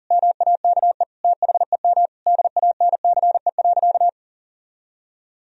Morse code, heard as a keyed tone, about 40 wpm, 700 Hz. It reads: MAKE THEM DANCE.
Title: Sentences from Top 1000 Words - Review Part 4 - 40wpm